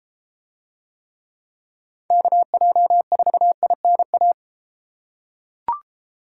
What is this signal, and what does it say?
Morse code, heard as a keyed tone, about 34 wpm, 700 Hz. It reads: KJ4INA